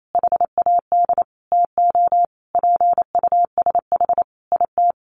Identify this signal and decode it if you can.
Morse code, heard as a keyed tone, about 28 wpm, 700 Hz. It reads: HAD TO PUSH IT
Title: Sentences from Top 1300 Words 28wpm